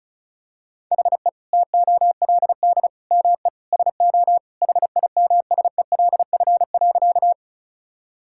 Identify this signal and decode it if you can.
Morse code, heard as a keyed tone, about 36 wpm, 700 Hz. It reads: HE TOLD ME SO HIMSELF.
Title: Sentences from Top 1600 Words 35wpm